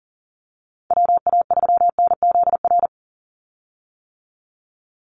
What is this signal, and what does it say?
Morse code, heard as a keyed tone, about 40 wpm, 700 Hz. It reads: WA3NZR
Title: Call Signs - 40wpm